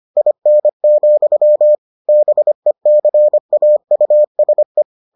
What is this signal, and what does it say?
Morse code, heard as a keyed tone, about 25 wpm, 600 Hz. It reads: IN, BECAUSE